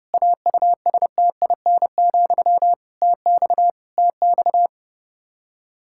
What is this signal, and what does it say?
Morse code, heard as a keyed tone, about 30 wpm, 700 Hz. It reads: AUSTIN, TX TX
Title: Practice QSO - Exchange 1 - Beginning - 30wpm